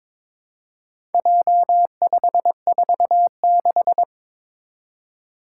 Morse code, heard as a keyed tone, about 22 wpm, 700 Hz. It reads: J546
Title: Four Challenging Letter-Number Combinations 22wpm